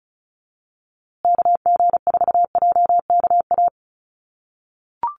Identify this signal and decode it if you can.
Morse code, heard as a keyed tone, about 34 wpm, 700 Hz. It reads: KG4JKA
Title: USA Call Signs Repeated 2x 35wpm